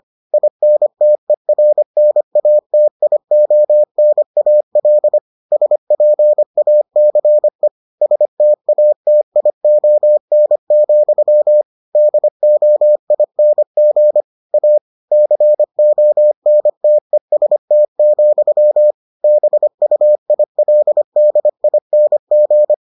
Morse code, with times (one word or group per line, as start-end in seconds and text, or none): 0.3-5.2 INTERNATIONAL
5.5-7.7 SPACE
8.0-11.6 STATION,
12.0-14.2 DOING
14.5-14.8 A
15.1-18.9 CONTEST,
19.2-22.7 BUILDING